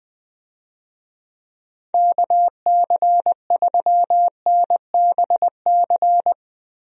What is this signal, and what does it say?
Morse code, heard as a keyed tone, about 20 wpm, 700 Hz. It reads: KC3NBC
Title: USA Call Signs - 6 Characters - Repeated 2x - 20wpm